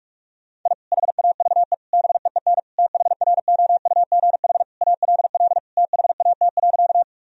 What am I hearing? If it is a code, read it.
Morse code, heard as a keyed tone, about 46 wpm, 700 Hz. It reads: I HAVE BEEN THROUGH ALL THAT.
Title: Sentences from Top 1000 Words - Review Part 1 - 45wpm